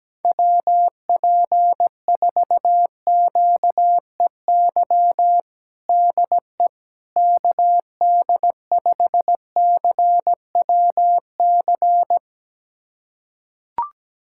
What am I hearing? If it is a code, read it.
Morse code, heard as a keyed tone, about 17 wpm, 700 Hz. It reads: WP4QEY DE KD5CWC
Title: USA Callsign DE Callsign 17wpm